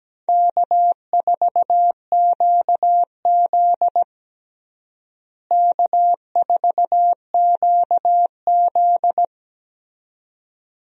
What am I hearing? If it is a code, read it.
Morse code, heard as a keyed tone, about 17 wpm, 700 Hz. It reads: K4QZ K4QZ
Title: USA Call Signs - 4 Characters - Repeated 2x - 17wpm